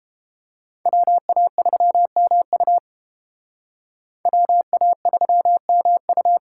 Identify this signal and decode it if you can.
Morse code, heard as a keyed tone, about 31 wpm, 700 Hz. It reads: WA3MU WA3MU